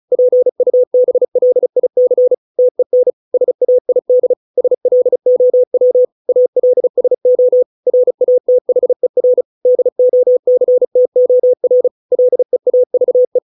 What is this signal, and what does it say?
Morse code, heard as a keyed tone, about 34 wpm, 500 Hz. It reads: PUBLIC TEN SAID SLOW ALSO RATHER DOCTOR LEAVE